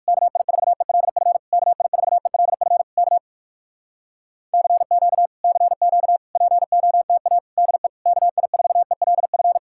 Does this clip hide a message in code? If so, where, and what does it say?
Yes: Morse code from 0.1 to 9.6 s, heard as a keyed tone, about 66 wpm, spaced as for 57 wpm, 700 Hz. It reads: KI4ELF KI4ELF K CQ CQ POTA DE KI4ELF